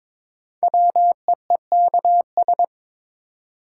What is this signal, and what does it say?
Morse code, heard as a keyed tone, about 21 wpm, 700 Hz. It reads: WEEKS
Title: Sets of 5 Words - New Word Drill - 22wpm